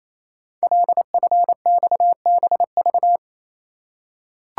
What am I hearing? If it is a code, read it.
Morse code, heard as a keyed tone, about 27 wpm, 700 Hz. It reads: LFXBV
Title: Code Groups - 5 Letters - 28wpm